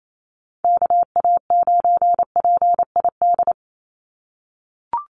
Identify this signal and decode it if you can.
Morse code, heard as a keyed tone, about 28 wpm, 700 Hz. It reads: KA9PID
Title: USA Call Signs 28wpm